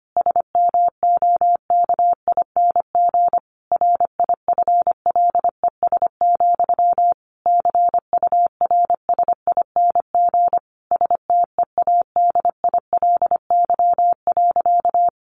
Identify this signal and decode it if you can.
Morse code, heard as a keyed tone, about 25 wpm, 700 Hz. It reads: SMOKING RIFLES, CURSING STEADILY.
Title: A Sound Of Thunder 25wpm